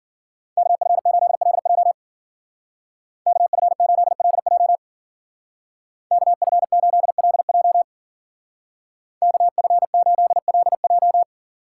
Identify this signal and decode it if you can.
Morse code, heard as a keyed tone, about 50 wpm, 700 Hz. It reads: KF8LJ KF8LJ KF8LJ KF8LJ